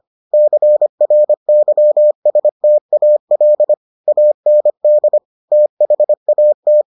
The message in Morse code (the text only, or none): CRYSTAL AND THAT